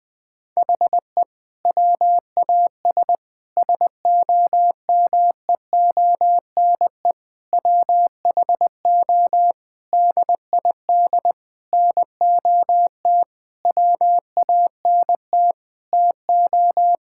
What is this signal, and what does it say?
Morse code, heard as a keyed tone, about 21 wpm, 700 Hz. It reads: HE WAS SOMEONE WHO DID NOT WANT TO
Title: Sentences with 6-Word Phrases - 20wpm